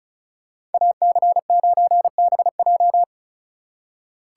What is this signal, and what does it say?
Morse code, heard as a keyed tone, about 34 wpm, 700 Hz. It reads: AC9BJ
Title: USA Call Signs - 5 Characters - 3x Character Spacing - 35wpm